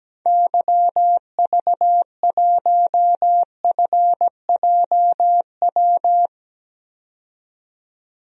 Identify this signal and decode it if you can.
Morse code, heard as a keyed tone, about 17 wpm, 700 Hz. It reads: YV1FJW
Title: DXCC-100 Call Signs - 6 Characters - 17wpm